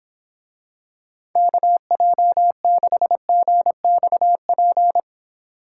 Morse code, heard as a keyed tone, about 26 wpm, 700 Hz. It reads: KJ6GXP